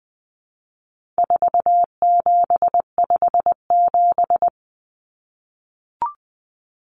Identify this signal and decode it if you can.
Morse code, heard as a keyed tone, about 20 wpm, 700 Hz. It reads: 4757
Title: Numbers - 4-Digit 20wpm